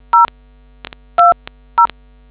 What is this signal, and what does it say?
DTMF, heard as telephone touch tones, tones of about 140 ms. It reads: 020